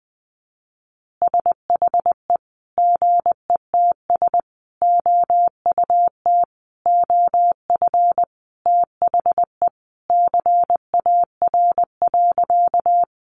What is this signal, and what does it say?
Morse code, heard as a keyed tone, about 20 wpm, 700 Hz. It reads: SHE GETS OUT OF THE CAR.
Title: Easy Phrases 20wpm